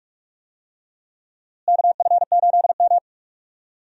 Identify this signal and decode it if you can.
Morse code, heard as a keyed tone, about 42 wpm, 700 Hz. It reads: KF8M
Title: USA Call Signs - 4 Characters - Repeated 2x - 45wpm